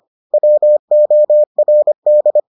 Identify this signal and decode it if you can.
Morse code, heard as a keyed tone, about 25 wpm, 600 Hz. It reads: WORD